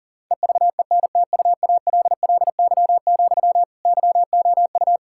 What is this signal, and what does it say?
Morse code, heard as a keyed tone, about 40 wpm, 700 Hz. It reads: EVENTUALLY, YOU
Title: Sentences from Top 1000 Words - Review Part 6 - 40wpm